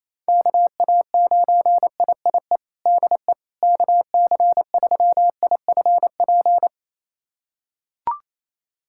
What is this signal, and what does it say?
Morse code, heard as a keyed tone, about 28 wpm, 700 Hz. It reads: KA9IIE DE KC3IFP
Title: USA Callsign DE Callsign 28wpm